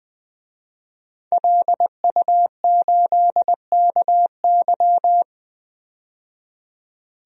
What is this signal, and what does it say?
Morse code, heard as a keyed tone, about 20 wpm, 700 Hz. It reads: LU8KY
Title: DXCC-100 Call Signs - 5 Characters - 20wpm